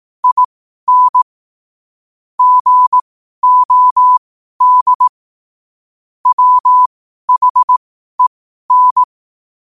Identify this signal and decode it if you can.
Morse code, heard as a keyed tone, about 18 wpm, spaced as for 12 wpm, 1000 Hz. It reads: IN GOD WHEN
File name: qotdfiles / 12wpm